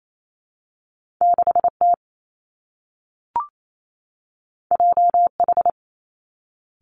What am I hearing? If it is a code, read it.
Morse code, heard as a keyed tone, about 28 wpm, 700 Hz. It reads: 6 T J H
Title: Two Challenging Letter-Number Combinations 28wpm